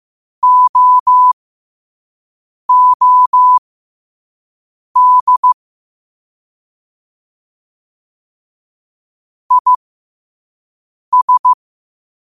Morse code, heard as a keyed tone, about 15 wpm, spaced as for 5 wpm, 1000 Hz. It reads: OOD IS
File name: qotdfiles / 5wpm